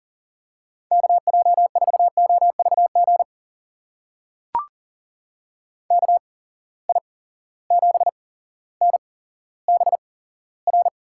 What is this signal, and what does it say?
Morse code, heard as a keyed tone, about 39 wpm, spaced as for 17 wpm, 700 Hz. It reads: KJ4OVG KI7NBR
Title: USA Call Signs - 6 Characters - 8x Character Spacing - 40wpm